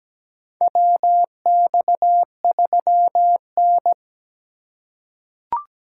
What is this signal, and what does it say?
Morse code, heard as a keyed tone, about 17 wpm, 700 Hz. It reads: WX3N
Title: USA Call Signs - 4x Character Spacing - 17wpm